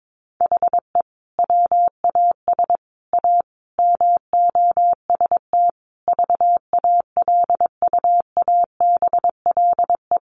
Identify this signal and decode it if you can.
Morse code, heard as a keyed tone, about 22 wpm, 700 Hz. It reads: HE WAS A MOST VALUABLE
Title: Sentences from Top 1200 Words 22wpm